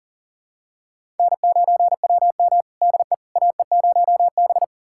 Morse code, heard as a keyed tone, about 40 wpm, 700 Hz. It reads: N9WM DE AE0B
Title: USA Callsign DE Callsign - 4 Characters - 40wpm